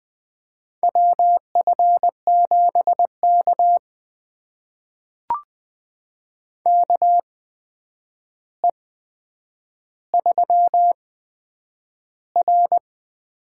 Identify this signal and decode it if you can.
Morse code, heard as a keyed tone, about 20 wpm, spaced as for 9 wpm, 700 Hz. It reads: WF7K KE3R
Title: USA Call Signs - 4 Characters - 8x Character Spacing - 20wpm